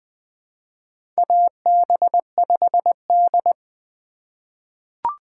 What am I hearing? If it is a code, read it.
Morse code, heard as a keyed tone, about 21 wpm, 700 Hz. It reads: AB5D
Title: USA Call Signs - 4 Characters - 3x Character Spacing - 20wpm